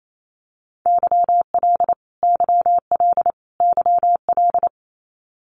Morse code, heard as a keyed tone, about 28 wpm, 700 Hz. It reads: YL YL YL
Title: Abbreviations Repeated 3x - 28wpm